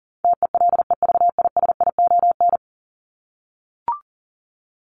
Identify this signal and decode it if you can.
Morse code, heard as a keyed tone, about 40 wpm, 700 Hz. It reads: TELEVISION
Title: Top 501-750 Words 40wpm